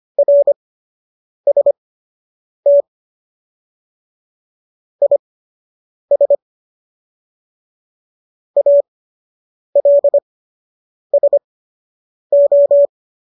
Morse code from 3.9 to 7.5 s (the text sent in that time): IS